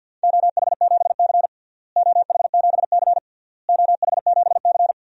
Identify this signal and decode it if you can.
Morse code, heard as a keyed tone, about 48 wpm, 700 Hz. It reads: OH7C OH7C OH7C
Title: DXCC-100 Call Signs - 4 Characters - Repeated 3x - 50wpm